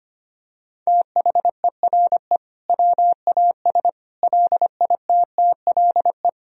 Morse code, heard as a keyed tone, about 25 wpm, 700 Hz. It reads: THERE WAS LITTLE
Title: Sentences from Top 1000 Words - Review Part 4 - 25wpm